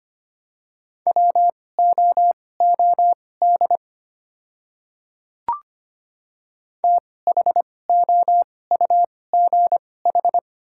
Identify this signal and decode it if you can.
Morse code, heard as a keyed tone, about 25 wpm, spaced as for 18 wpm, 700 Hz. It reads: WOOD THOUGH
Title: 2x Character Spacing - Top 500 Words - 25wpm